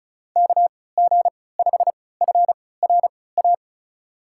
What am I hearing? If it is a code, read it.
Morse code, heard as a keyed tone, about 36 wpm, spaced as for 20 wpm, 700 Hz. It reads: KG5FRA
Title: USA Call Signs - 3x Character Spacing - 35wpm